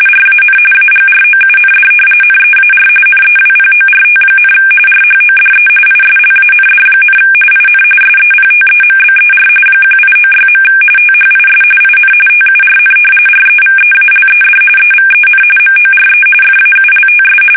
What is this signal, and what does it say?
RTTY, about 45.45 baud. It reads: RAN AGROUND NEAR NEWPORT, RHODE ISLAND, IN SEPTEMBER 1896. THIS IN TURN REVEALED EVEN MORE FAULTS
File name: fsk850 / battleships